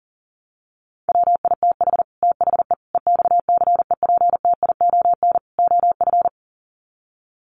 Morse code, heard as a keyed tone, about 38 wpm, 700 Hz. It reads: WITH THE EXCEPTION OF
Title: Sets of 4 Words 40wpm